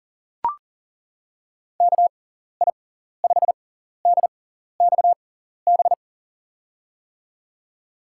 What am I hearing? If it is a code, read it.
Morse code, heard as a keyed tone, about 40 wpm, spaced as for 14 wpm, 700 Hz. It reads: KI5DXB
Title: USA Call Signs - 6 Characters - 6x Character Spacing - 40wpm